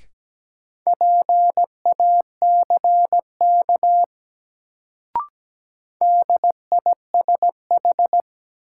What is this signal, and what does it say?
Morse code, heard as a keyed tone, about 17 wpm, 700 Hz. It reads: PACK DISH
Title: Sentences from Top 1300 Words - New Word Drill - 17wpm